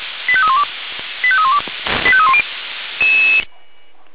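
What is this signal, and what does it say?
Five-tone selective call (ZVEI), heard as quick runs of five tones, tones of about 70 ms. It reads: 85312 85312 85310
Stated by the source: Received on SDR, ZVEI five-tone call.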